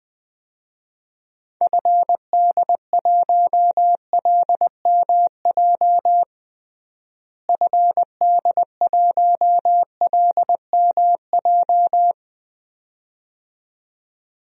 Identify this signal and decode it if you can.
Morse code, heard as a keyed tone, about 20 wpm, 700 Hz. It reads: FD1LMJ FD1LMJ
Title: DXCC-100 Call Signs - 6 Characters - Repeated 2x - 20wpm